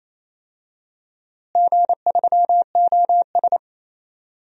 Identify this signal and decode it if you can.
Morse code, heard as a keyed tone, about 28 wpm, 700 Hz. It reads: G3OS
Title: DXCC-100 Call Signs - 4 Characters - Repeated 2x - 28wpm